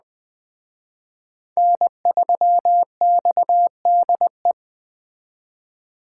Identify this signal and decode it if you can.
Morse code, heard as a keyed tone, about 20 wpm, 700 Hz. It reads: N3XDE
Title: Speed Racing - USA 1x3 Call Signs 17wpm